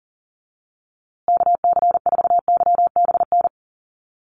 Morse code, heard as a keyed tone, about 40 wpm, 700 Hz. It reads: KC4YBN
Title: USA Call Signs Repeated 3x 40wpm